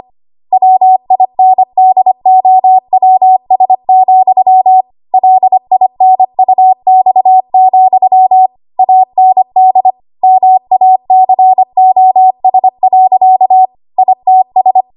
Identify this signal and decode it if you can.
Morse code, heard as a keyed tone, about 25 wpm, 750 Hz. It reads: WINDOWS, LINUX, AND MACOS. ITS
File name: Archive / 25 WPM / 25WPM